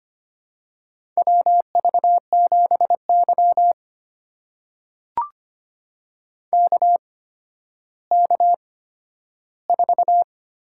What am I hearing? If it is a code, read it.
Morse code, heard as a keyed tone, about 25 wpm, spaced as for 12 wpm, 700 Hz. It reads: WV7Y KK4
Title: USA Call Signs - 8x Character Spacing - 25wpm